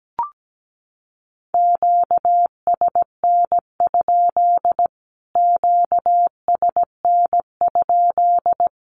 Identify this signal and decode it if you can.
Morse code, heard as a keyed tone, about 17 wpm, 700 Hz. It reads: QSN? QSN?
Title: Q-Codes Repeated 3x 17wpm